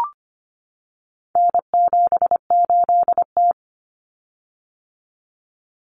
Morse code, heard as a keyed tone, about 25 wpm, 700 Hz. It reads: N78T
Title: Common Cut Numbers - 4-Digit 25wpm